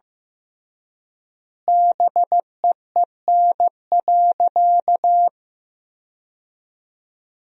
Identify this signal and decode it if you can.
Morse code, heard as a keyed tone, about 15 wpm, 700 Hz. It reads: BEEN.
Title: 3x Word-Spacing - Sentences from Top 300 Words - 15wpm